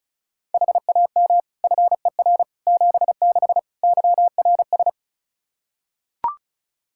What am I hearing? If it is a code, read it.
Morse code, heard as a keyed tone, about 36 wpm, 700 Hz. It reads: HAM FER 76 YRS
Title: Practice QSO - Exchange 3 - More Info - 35wpm